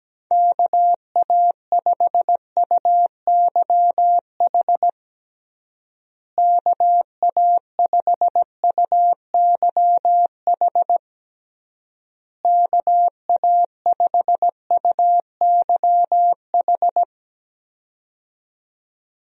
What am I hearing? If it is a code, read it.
Morse code, heard as a keyed tone, about 17 wpm, 700 Hz. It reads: KA5UYH KA5UYH KA5UYH
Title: USA Call Signs - 6 Characters - Repeated 3x - 17wpm